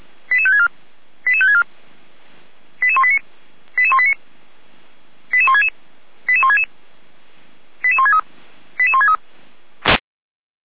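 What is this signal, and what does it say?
Five-tone selective call (ZVEI), heard as quick runs of five tones, tones of about 80 ms. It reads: 88564 88564 88189 88189 88166 88166 88163 88163
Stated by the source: ZVEI five-tone call 88355, received by SDR.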